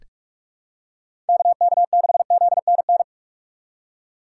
Morse code, heard as a keyed tone, about 46 wpm, 700 Hz. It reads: KK6ZNN
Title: USA Call Signs - 6 Characters - 6x Character Spacing - 45wpm